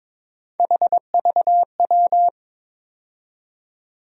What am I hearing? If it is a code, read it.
Morse code, heard as a keyed tone, about 22 wpm, 700 Hz. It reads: HVW
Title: USA Call Sign Postfixes - 3 Letters 22wpm